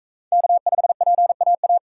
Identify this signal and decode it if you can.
Morse code, heard as a keyed tone, about 40 wpm, 700 Hz. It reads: K5PAA